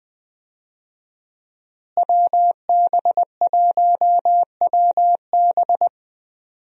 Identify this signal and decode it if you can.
Morse code, heard as a keyed tone, about 20 wpm, 700 Hz. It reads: WB1WB
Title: USA Call Signs - 5 Characters - Repeated 2x - 20wpm